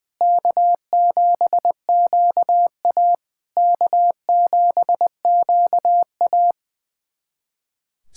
Morse code, heard as a keyed tone, about 20 wpm, 700 Hz. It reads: K7QA K7QA
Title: SOTA QSO Practice 20wpm